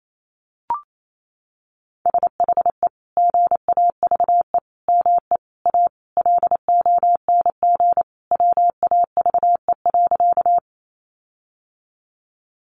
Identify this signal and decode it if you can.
Morse code, heard as a keyed tone, about 28 wpm, 700 Hz. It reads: SHE GAVE ME A LONG WAVE.
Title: Sentences from Top 1400 Words 28wpm